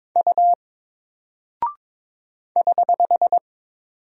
Morse code, heard as a keyed tone, about 22 wpm, 700 Hz. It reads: U <HH>
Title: Single Character (Letter:Number:Punctuation:Prosign) Alt 22wpm